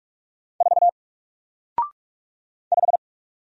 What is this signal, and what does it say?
Morse code, heard as a keyed tone, about 44 wpm, 700 Hz. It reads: 4 5
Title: Single Letter-Number Alt 45wpm